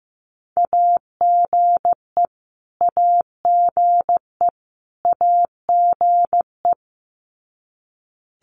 Morse code, heard as a keyed tone, about 15 wpm, 700 Hz. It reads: AGE AGE AGE
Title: Top 500 Words Repeated 15wpm